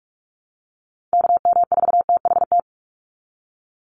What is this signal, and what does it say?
Morse code, heard as a keyed tone, about 44 wpm, 700 Hz. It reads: KM4THT
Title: USA Call Signs 45wpm